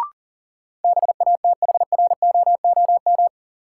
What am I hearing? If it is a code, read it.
Morse code, heard as a keyed tone, about 40 wpm, 700 Hz. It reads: BATHROOM
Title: Sets of 5 Words Encore - New Word Drill - 40wpm